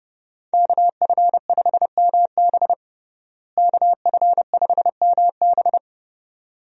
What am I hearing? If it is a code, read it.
Morse code, heard as a keyed tone, about 32 wpm, 700 Hz. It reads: KF5MB KF5MB
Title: USA Call Signs - 5 Characters - Repeated 2x - 30wpm